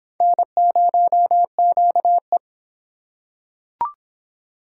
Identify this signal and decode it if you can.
Morse code, heard as a keyed tone, about 25 wpm, 700 Hz. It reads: N0QE